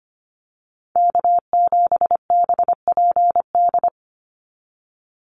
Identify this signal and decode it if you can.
Morse code, heard as a keyed tone, about 25 wpm, 700 Hz. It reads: K7BPD
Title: USA Call Signs 25wpm